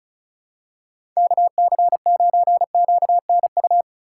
Morse code, heard as a keyed tone, about 36 wpm, 700 Hz. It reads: KC9QNU
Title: USA Call Signs - 6 Characters - 2x Character Spacing - 35wpm